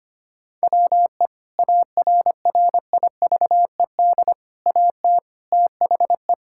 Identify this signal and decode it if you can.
Morse code, heard as a keyed tone, about 25 wpm, 700 Hz. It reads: WE ARRIVED AT THE
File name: practice / 25wpm / Sentences with 3-Word Phrases - 25wpm